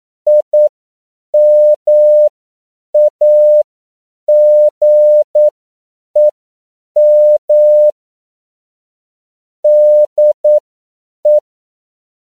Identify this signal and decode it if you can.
Morse code, heard as a keyed tone, about 9 wpm, spaced as for 7 wpm, 600 Hz. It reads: IMAGEM DE